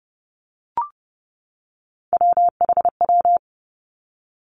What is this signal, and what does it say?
Morse code, heard as a keyed tone, about 32 wpm, 700 Hz. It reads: WHW
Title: Three Challenging Letter-Number Combinations 30wpm